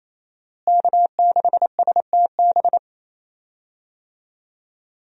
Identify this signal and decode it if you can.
Morse code, heard as a keyed tone, about 28 wpm, 700 Hz. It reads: K6STB
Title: USA Call Signs - 5 Characters - Repeated 3x - 28wpm